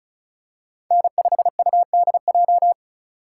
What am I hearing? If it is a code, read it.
Morse code, heard as a keyed tone, about 34 wpm, 700 Hz. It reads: N5UDJ
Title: USA Call Signs - 5 Characters - 4x Character Spacing - 35wpm